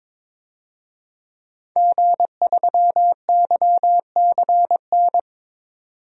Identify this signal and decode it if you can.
Morse code, heard as a keyed tone, about 22 wpm, 700 Hz. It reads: G3YCN
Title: DXCC-100 Call Signs - 5 Characters - 22wpm